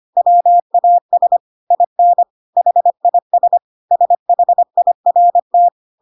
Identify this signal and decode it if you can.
Morse code, heard as a keyed tone, about 25 wpm, 700 Hz. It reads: WAS IN HIS SHIRT